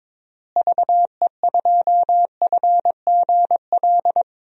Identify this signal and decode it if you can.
Morse code, heard as a keyed tone, about 22 wpm, 700 Hz. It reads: VE2FGL
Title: DXCC-100 Call Signs - 6 Characters - Repeated 3x - 22wpm